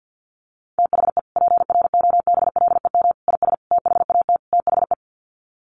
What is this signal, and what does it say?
Morse code, heard as a keyed tone, about 50 wpm, 700 Hz. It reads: THE PROBLEM IS THAT THE
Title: Sets of 5 Words Encore2 50wpm